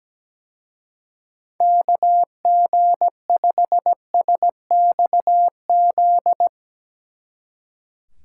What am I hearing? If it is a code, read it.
Morse code, heard as a keyed tone, about 18 wpm, 700 Hz. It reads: KG5SXZ
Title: Speed Racing - USA 2x3 Call Signs 17wpm